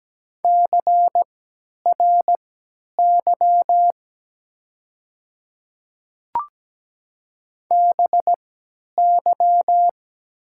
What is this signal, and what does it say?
Morse code, heard as a keyed tone, about 17 wpm, spaced as for 10 wpm, 700 Hz. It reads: CRY BY
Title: 3x Character Spacing - Top 500 Words - 17wpm